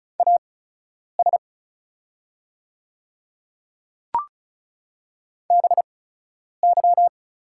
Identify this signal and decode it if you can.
Morse code, heard as a keyed tone, about 36 wpm, spaced as for 10 wpm, 700 Hz. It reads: AS BY
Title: 8x Character Spacing - Top 100 Words - 35wpm